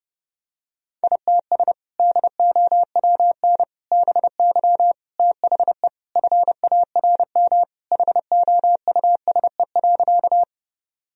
Morse code, heard as a keyed tone, about 32 wpm, 700 Hz. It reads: ITS DOWN BY THE FARM HOUSE.
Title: Sentences from Top 1000 Words - Review Part 2 - 30wpm